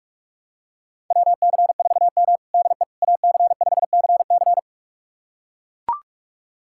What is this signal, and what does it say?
Morse code, heard as a keyed tone, about 42 wpm, 700 Hz. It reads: WC4M DE AC5CC
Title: USA Callsign DE Callsign 45wpm